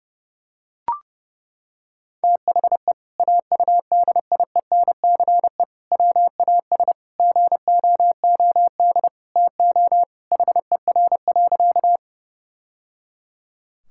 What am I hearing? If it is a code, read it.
Morse code, heard as a keyed tone, about 30 wpm, 700 Hz. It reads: THE AUDIENCE WAS GOOD TO HER.
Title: Sentences from Top 1000 Words - Review Part 5 - 30wpm